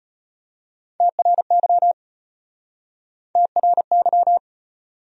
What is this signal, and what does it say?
Morse code, heard as a keyed tone, about 34 wpm, 700 Hz. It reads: TRY TRY